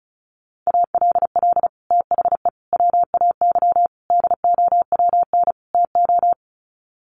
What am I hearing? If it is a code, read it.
Morse code, heard as a keyed tone, about 34 wpm, 700 Hz. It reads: ALL THE WAY DOWN TO
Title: Sets of 5 Words Encore2 35wpm